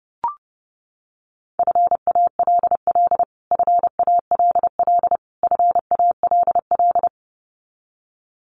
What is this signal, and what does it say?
Morse code, heard as a keyed tone, about 30 wpm, 700 Hz. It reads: FALL FALL FALL
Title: Top 500 Words Repeated 30wpm